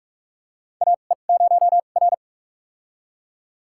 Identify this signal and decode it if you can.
Morse code, heard as a keyed tone, about 42 wpm, spaced as for 31 wpm, 700 Hz. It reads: AE0R
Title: USA Call Signs - 4 Characters - 2x Character Spacing - 45wpm